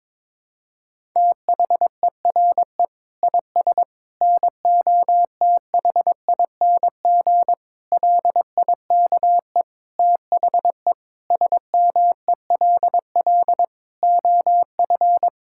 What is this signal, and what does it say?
Morse code, heard as a keyed tone, about 22 wpm, 700 Hz. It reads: THERE IS NOTHING LIKE THE SMELL OF
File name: Difficult Sentences with Hints - 22wpm